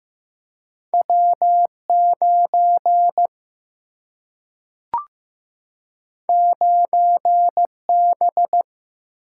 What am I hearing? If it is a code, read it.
Morse code, heard as a keyed tone, about 15 wpm, 700 Hz. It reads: W 9 9 B
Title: Single-Letter Single-Number 15wpm